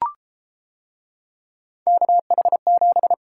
Morse code, heard as a keyed tone, about 31 wpm, 700 Hz. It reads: KH7